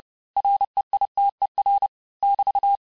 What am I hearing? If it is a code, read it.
Morse code, heard as a keyed tone, about 30 wpm, 800 Hz. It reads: REITER <BT>
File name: cours-du-mercredi / audios / cours-854-mots-vitesse-28-mpm